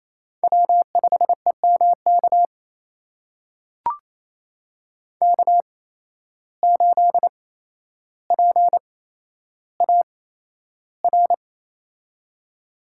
Morse code, heard as a keyed tone, about 28 wpm, spaced as for 12 wpm, 700 Hz. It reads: W5EMK K8PAR
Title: USA Call Signs - 5 Characters - 8x Character Spacing - 28wpm